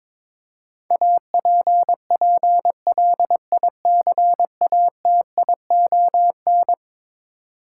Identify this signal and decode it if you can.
Morse code, heard as a keyed tone, about 22 wpm, 700 Hz. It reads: APPLICATION
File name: practice / 22wpm / Eleven Letter Words - 22wpm